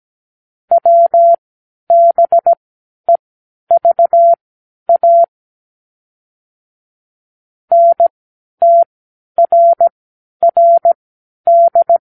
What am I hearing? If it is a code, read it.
Morse code, heard as a keyed tone, about 17 wpm, spaced as for 10 wpm, 680 Hz. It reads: WBEVA NTRRD
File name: k7qo / Lesson 36 (A-X)